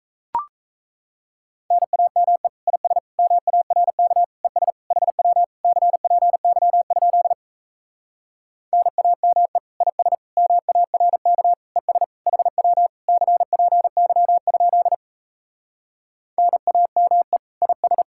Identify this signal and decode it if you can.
Morse code, heard as a keyed tone, about 38 wpm, 700 Hz. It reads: NAME IS MARK ES HW CPY? NAME IS MARK ES HW CPY? NAME IS